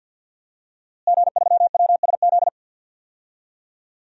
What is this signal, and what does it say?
Morse code, heard as a keyed tone, about 51 wpm, 700 Hz. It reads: G3WSZ
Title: DXCC-100 Call Signs - 5 Characters - Repeated 3x - 50wpm